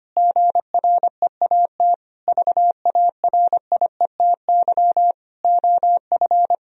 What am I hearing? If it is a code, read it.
Morse code, heard as a keyed tone, about 25 wpm, 700 Hz. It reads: GREAT VARIETY OF
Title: Sentences from Top 1000 Words - Review Part 4 - 25wpm